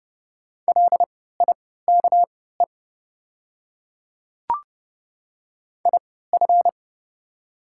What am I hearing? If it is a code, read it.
Morse code, heard as a keyed tone, about 30 wpm, spaced as for 17 wpm, 700 Hz. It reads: LIKE IF
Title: 3x Character Spacing - Top 100 Words - 30wpm